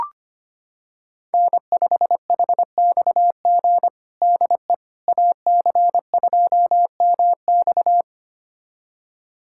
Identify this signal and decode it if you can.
Morse code, heard as a keyed tone, about 25 wpm, 700 Hz. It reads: N5HXG DE AC2MX
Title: USA Callsign DE Callsign - 5 Characters - 25wpm